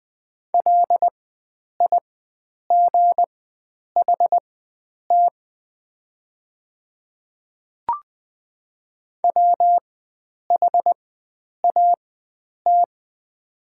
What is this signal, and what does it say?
Morse code, heard as a keyed tone, about 20 wpm, spaced as for 9 wpm, 700 Hz. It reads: LIGHT WHAT
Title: 4x Character Spacing - Top 300 Words - 20wpm